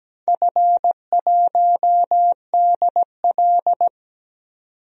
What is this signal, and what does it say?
Morse code, heard as a keyed tone, about 17 wpm, 700 Hz. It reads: F1DL
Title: DXCC-100 Call Signs - 4 Characters - 17wpm